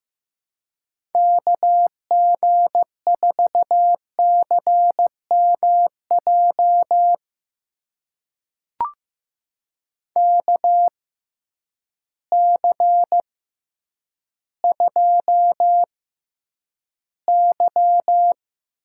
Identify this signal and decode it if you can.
Morse code, heard as a keyed tone, about 15 wpm, spaced as for 9 wpm, 700 Hz. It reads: KG4CMJ KC2Y
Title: USA Call Signs - 6 Characters - 6x Character Spacing - 15wpm